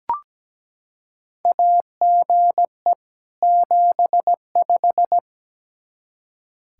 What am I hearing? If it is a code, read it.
Morse code, heard as a keyed tone, about 17 wpm, 700 Hz. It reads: AGE 75
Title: K1USN SST QSO Practice 17wpm